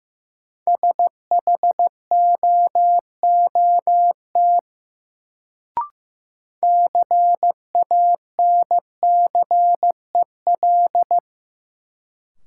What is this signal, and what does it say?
Morse code, heard as a keyed tone, about 15 wpm, 700 Hz. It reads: SHOOT CANCEL
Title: Sentences from Top 1200 Words - New Word Drill Alt - 15wpm